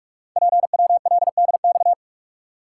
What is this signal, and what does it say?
Morse code, heard as a keyed tone, about 44 wpm, 700 Hz. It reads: PWLDX
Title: Code Groups - 5 Letter:Numbers - 45wpm